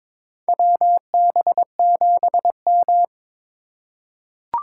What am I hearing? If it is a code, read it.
Morse code, heard as a keyed tone, about 22 wpm, 700 Hz. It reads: WB7M
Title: USA Call Signs - 4 Characters - 6x Character Spacing - 22wpm